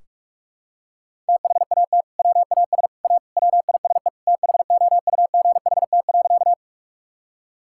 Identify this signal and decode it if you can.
Morse code, heard as a keyed tone, about 46 wpm, 700 Hz. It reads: THAT WAS A WISE THOUGHT.
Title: Sentences from Top 1600 Words 45wpm